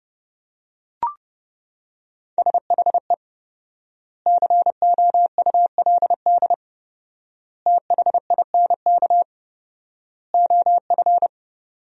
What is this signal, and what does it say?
Morse code, heard as a keyed tone, about 30 wpm, 700 Hz. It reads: SHE COULD THINK OF
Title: 4x Word-Spacing - Sentences from Top 500 Words - 30wpm